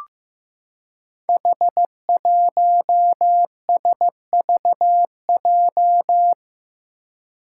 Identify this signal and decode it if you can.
Morse code, heard as a keyed tone, about 15 wpm, 700 Hz. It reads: H1SVJ
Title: Five Challenging Letter-Number Combinations 15wpm